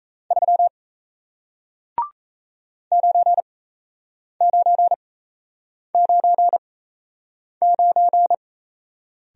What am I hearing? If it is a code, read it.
Morse code, heard as a keyed tone, about 38 wpm, spaced as for 8 wpm, 700 Hz. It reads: 3 9999